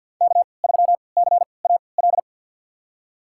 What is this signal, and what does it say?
Morse code, heard as a keyed tone, about 48 wpm, spaced as for 28 wpm, 700 Hz. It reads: K3CAL
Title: USA Call Signs - 3x Character Spacing - 50wpm